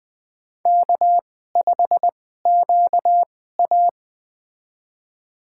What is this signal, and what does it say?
Morse code, heard as a keyed tone, about 20 wpm, spaced as for 15 wpm, 700 Hz. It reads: K5QA
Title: USA Call Signs - 2x Character Spacing - 20wpm